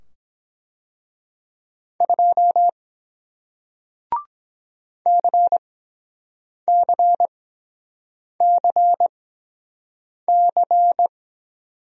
Morse code, heard as a keyed tone, about 23 wpm, spaced as for 7 wpm, 700 Hz. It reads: 2 CCCC